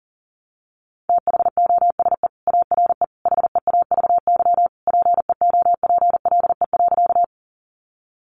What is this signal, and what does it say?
Morse code, heard as a keyed tone, about 40 wpm, 700 Hz. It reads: THOSE ARE HEAVY PEOPLE.
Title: Sentences from Top 1100 Words 40wpm